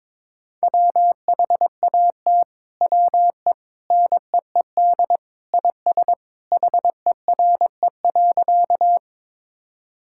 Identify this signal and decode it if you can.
Morse code, heard as a keyed tone, about 22 wpm, 700 Hz. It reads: WHAT WE NEED IS HERE.
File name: practice / 22wpm / Sentences with 2-Word Phrases - 22wpm